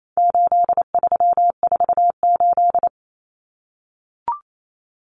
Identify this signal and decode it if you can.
Morse code, heard as a keyed tone, about 28 wpm, 700 Hz. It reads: 8348
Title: Common Cut Numbers - 4-Digit 28wpm